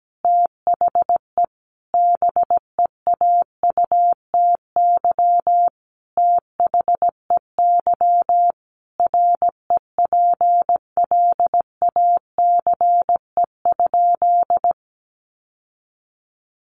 Morse code, heard as a keyed tone, about 17 wpm, 700 Hz. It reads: THE BEAUTY THEY REPLACE?
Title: Sentences from Top 1400 Words 17wpm